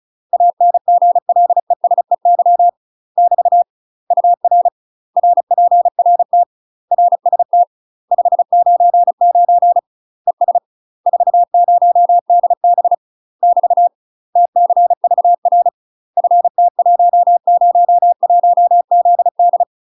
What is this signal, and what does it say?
Morse code, heard as a keyed tone, about 37 wpm, 700 Hz. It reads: ANGLESEY <BT> UR RPRT RST 599 ES 40DB <BT> TCVR FT101ZD
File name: UK MASTER QSOs TRAINING / 35WPM / UK MASTER QSOs 0019